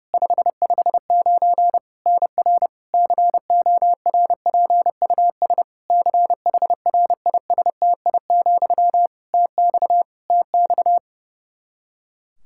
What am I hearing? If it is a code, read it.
Morse code, heard as a keyed tone, about 30 wpm, 700 Hz. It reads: 559 NR CORPUS CHRISTI, TX TX
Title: K1USN SST QSO Practice 30wpm